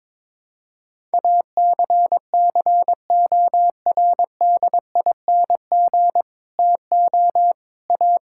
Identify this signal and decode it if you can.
Morse code, heard as a keyed tone, about 22 wpm, 700 Hz. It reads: ACCORDING TO A
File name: Sets of 9 Words - 22wpm